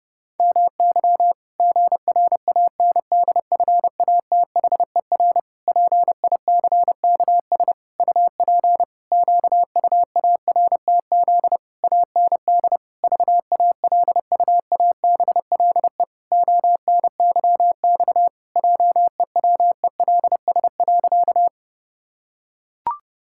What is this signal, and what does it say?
Morse code, heard as a keyed tone, about 30 wpm, 700 Hz. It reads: MY GRANDFATHER PICKS UP QUARTZ AND VALUABLE ONYX JEWELS.
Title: Pangram Challenge v2 - 30wpm